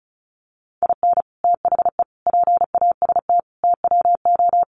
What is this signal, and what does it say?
Morse code, heard as a keyed tone, about 34 wpm, 700 Hz. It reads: IN THE PAST TWO
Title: Sets of 5 Words 35wpm